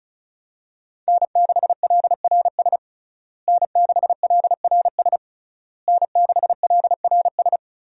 Morse code, heard as a keyed tone, about 34 wpm, 700 Hz. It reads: N6LRS N6LRS N6LRS
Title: USA Call Signs - 5 Characters - Repeated 3x - 35wpm